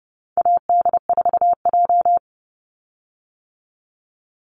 Morse code, heard as a keyed tone, about 30 wpm, 700 Hz. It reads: AD4J
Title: Call Signs - 30wpm